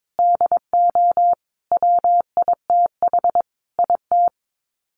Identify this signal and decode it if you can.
Morse code, heard as a keyed tone, about 22 wpm, 700 Hz. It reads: DO WITH IT
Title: Sets of 5 Words Encore 22wpm